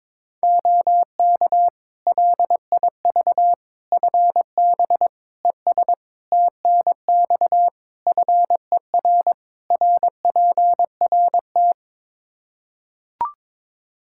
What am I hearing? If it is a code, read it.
Morse code, heard as a keyed tone, about 22 wpm, 700 Hz. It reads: OK LIV FB ES TNX FER RPRT
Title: Practice QSO - Exchange 2 - Info - 22wpm